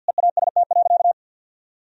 Morse code, heard as a keyed tone, about 50 wpm, 700 Hz. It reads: EAST.
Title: Sentences from Top 1000 Words - Review Part 3 - 50wpm